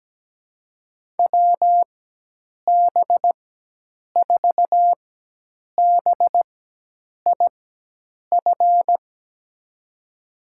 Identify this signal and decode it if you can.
Morse code, heard as a keyed tone, about 17 wpm, spaced as for 8 wpm, 700 Hz. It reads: WB4BIF
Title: USA Call Signs - 6 Characters - 4x Character Spacing - 17wpm